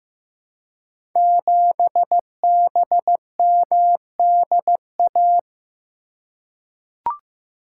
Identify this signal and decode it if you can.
Morse code, heard as a keyed tone, about 15 wpm, 700 Hz. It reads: 7BMDA
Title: Code Groups - 5 Letter:Numbers - 15wpm